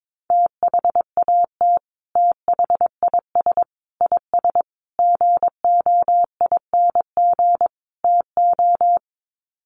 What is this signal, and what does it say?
Morse code, heard as a keyed tone, about 22 wpm, 700 Hz. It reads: THAT THIS IS GOING TO
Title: Sets of 5 Words Encore2 22wpm